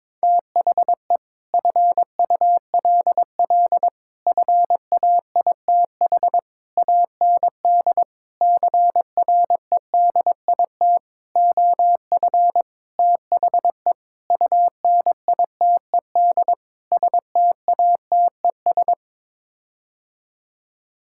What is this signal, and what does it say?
Morse code, heard as a keyed tone, about 21 wpm, 700 Hz. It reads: THE FULL FAITH AND CREDIT OF THE UNITED STATES
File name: Sets of 9 Words - 22wpm